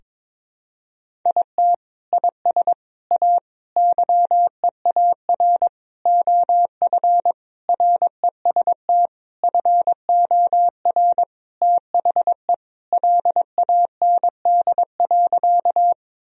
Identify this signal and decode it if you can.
Morse code, heard as a keyed tone, about 22 wpm, 700 Hz. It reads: IT IS A YEAR OF REST FOR THE LAND.
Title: Sentences from Top 1000 Words - Review Part 1 - 22wpm